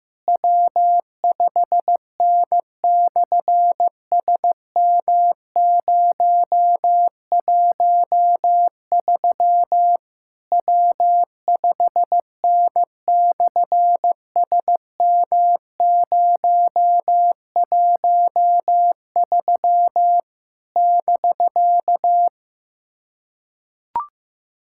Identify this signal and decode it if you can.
Morse code, heard as a keyed tone, about 15 wpm, 700 Hz. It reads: W5N/SM013 W5N/SM013 <BK>
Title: SOTA QSO Practice 15wpm